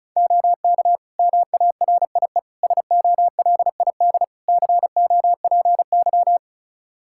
Morse code, heard as a keyed tone, about 36 wpm, 700 Hz. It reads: OK MARIE SOLID COPY
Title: Practice QSO - Exchange 3 - More Info - 35wpm